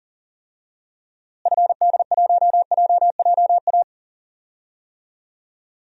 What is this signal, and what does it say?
Morse code, heard as a keyed tone, about 37 wpm, 700 Hz. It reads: FD1JJA
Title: DXCC-100 Call Signs - 6 Characters - Repeated 3x - 40wpm